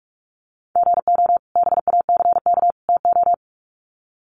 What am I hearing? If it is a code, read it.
Morse code, heard as a keyed tone, about 44 wpm, 700 Hz. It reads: GO BACK TO
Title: Sets of 3 Words 45wpm